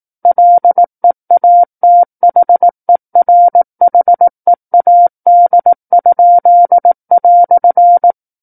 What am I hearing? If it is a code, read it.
Morse code, heard as a keyed tone, about 18 wpm, 700 Hz. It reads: LEATHERHEAD?"
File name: 18 WPM